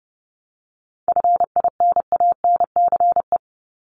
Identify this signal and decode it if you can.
Morse code, heard as a keyed tone, about 30 wpm, 700 Hz. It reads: FINANCE
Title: Top 1251-1500 Words 30wpm